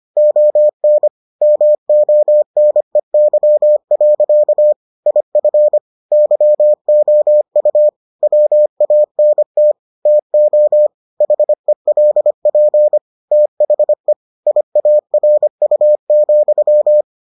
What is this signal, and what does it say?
Morse code, heard as a keyed tone, about 25 wpm, 600 Hz. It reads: ON MONEY. IF YOU WANT TO HELP THE IARU,